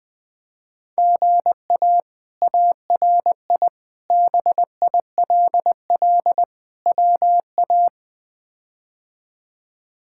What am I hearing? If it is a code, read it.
Morse code, heard as a keyed tone, about 20 wpm, 700 Hz. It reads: GA ARI BILL WA
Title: K1USN SST QSO Practice - 20wpm